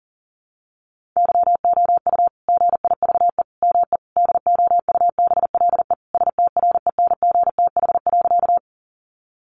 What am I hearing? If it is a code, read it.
Morse code, heard as a keyed tone, about 40 wpm, 700 Hz. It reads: YOU GIVE ME DOUBLE STRENGTH.
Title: Sentences from Top 1400 Words 40wpm